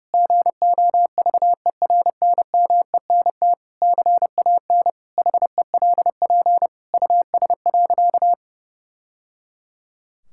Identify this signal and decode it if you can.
Morse code, heard as a keyed tone, about 32 wpm, 700 Hz. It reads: GOVERNMENT CAN HELP US.
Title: Sentences from Top 1000 Words - Review Part 5 - 30wpm